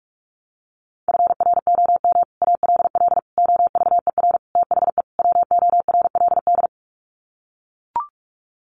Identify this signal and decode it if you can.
Morse code, heard as a keyed tone, about 44 wpm, 700 Hz. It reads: FROM ALL OVER THE WORLD
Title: Sets of 5 Words 45wpm